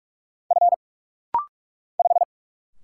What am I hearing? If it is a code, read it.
Morse code, heard as a keyed tone, about 45 wpm, 700 Hz. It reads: F 5
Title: Single Letter-Number Alt - Rapid-Fire 45wpm